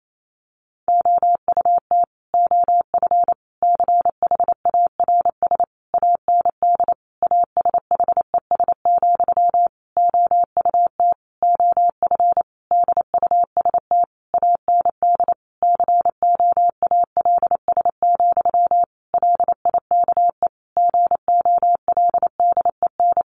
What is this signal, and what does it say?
Morse code, heard as a keyed tone, about 28 wpm, 700 Hz. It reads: OUT OF CHARS AND ASHES, OUT OF DUST AND COALS, LIKE GOLDEN
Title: A Sound Of Thunder 28wpm